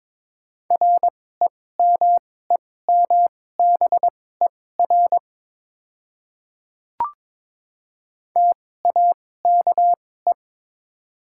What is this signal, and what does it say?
Morse code, heard as a keyed tone, about 21 wpm, spaced as for 16 wpm, 690 Hz. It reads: REMEMBER TAKE
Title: 2x Character Spacing - Top 500 Words - 22wpm